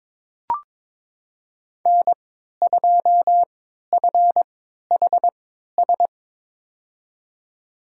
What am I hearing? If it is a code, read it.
Morse code, heard as a keyed tone, about 22 wpm, spaced as for 13 wpm, 700 Hz. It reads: N2FHS
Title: USA Call Signs - 5 Characters - 3x Character Spacing - 22wpm